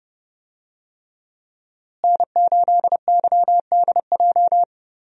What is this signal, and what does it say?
Morse code, heard as a keyed tone, about 30 wpm, 700 Hz. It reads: N8YDJ
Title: USA Call Signs - 5 Characters - Repeated 3x - 30wpm